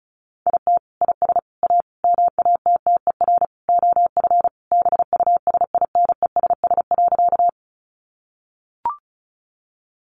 Morse code, heard as a keyed tone, about 34 wpm, 700 Hz. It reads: IT IS A MATTER OF BUSINESS.
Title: Sentences from Top 700 Words 35wpm